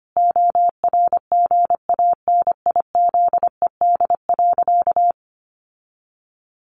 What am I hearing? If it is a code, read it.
Morse code, heard as a keyed tone, about 25 wpm, 700 Hz. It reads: ORGANIZED.
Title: Sentences from Top 1300 Words 25wpm